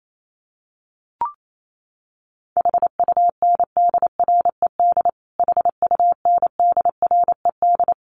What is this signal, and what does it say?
Morse code, heard as a keyed tone, about 28 wpm, 700 Hz. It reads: HUNDRED HUNDRED
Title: Top 500 Words Repeated 28wpm